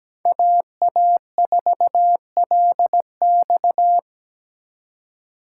Morse code, heard as a keyed tone, about 20 wpm, 700 Hz. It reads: AA4LX
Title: Speed Racing - USA 2x2 Call Signs 17wpm